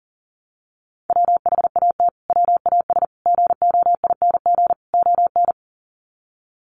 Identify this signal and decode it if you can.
Morse code, heard as a keyed tone, about 40 wpm, 700 Hz. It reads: WHAT WAS GOING ON
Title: Sets of 4 Words 40wpm